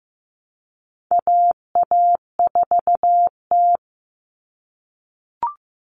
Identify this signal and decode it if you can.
Morse code, heard as a keyed tone, about 15 wpm, 700 Hz. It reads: AA4T
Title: All Cut Numbers - 4-Digit 15wpm